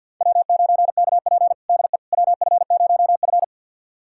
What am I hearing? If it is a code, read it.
Morse code, heard as a keyed tone, about 52 wpm, 700 Hz. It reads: W9KP DE WF0F
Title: USA Callsign DE Callsign - 4 Characters - 50wpm